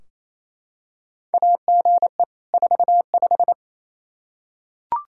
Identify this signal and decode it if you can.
Morse code, heard as a keyed tone, about 28 wpm, 700 Hz. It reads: AGE 45
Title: Practice QSO - Exchange 3 - More Info - 28wpm